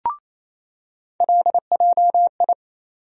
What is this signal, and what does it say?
Morse code, heard as a keyed tone, about 28 wpm, 700 Hz. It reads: LJI
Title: USA Call Sign Postfixes - 3 Letters 28wpm